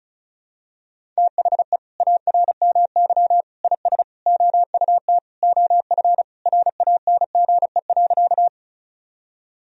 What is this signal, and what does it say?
Morse code, heard as a keyed tone, about 36 wpm, 700 Hz. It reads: THE ARMY IS OUT OF RANGE.
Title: Sentences from Top 1000 Words - Review Part 4 - 35wpm